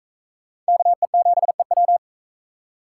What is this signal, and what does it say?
Morse code, heard as a keyed tone, about 43 wpm, 700 Hz. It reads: KE7EW